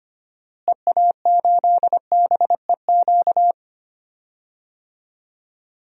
Morse code, heard as a keyed tone, about 25 wpm, 700 Hz. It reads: EA8BEQ
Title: DXCC-100 Call Signs - 6 Characters - Repeated 2x - 25wpm